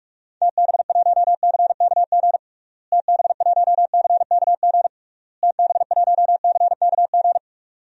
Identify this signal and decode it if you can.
Morse code, heard as a keyed tone, about 46 wpm, 700 Hz. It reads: TB1CKG TB1CKG TB1CKG
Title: DXCC-100 Call Signs - 6 Characters - Repeated 3x - 45wpm